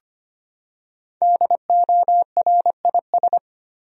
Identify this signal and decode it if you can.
Morse code, heard as a keyed tone, about 25 wpm, 700 Hz. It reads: DORIS
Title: USA Common Names - Max 5 Letters - 25wpm